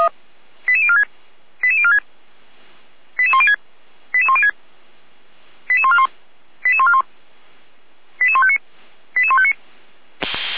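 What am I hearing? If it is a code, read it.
Five-tone selective call (ZVEI), heard as quick runs of five tones, tones of about 70 ms. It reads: 88046 88046 88106 88106 88141 88141 88159 88159
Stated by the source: ZVEI five-tone selective call received by SDR, from the Feuerwehr Orlamünde fire brigade alerting archive.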